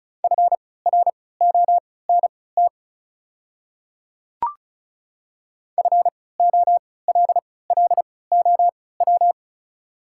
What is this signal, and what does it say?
Morse code, heard as a keyed tone, about 36 wpm, spaced as for 20 wpm, 700 Hz. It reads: FRONT FOLLOW
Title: 3x Character Spacing - Top 500 Words - 35wpm